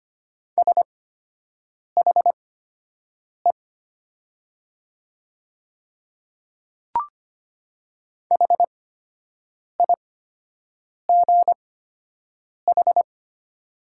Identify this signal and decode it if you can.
Morse code, heard as a keyed tone, about 25 wpm, spaced as for 7 wpm, 700 Hz. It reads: SHE HIGH
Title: 8x Character Spacing - Top 200 Words - 25wpm